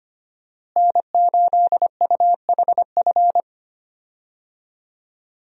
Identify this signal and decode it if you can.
Morse code, heard as a keyed tone, about 25 wpm, 700 Hz. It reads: N8UHF
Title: USA Call Signs - 5 Characters - Repeated 2x - 25wpm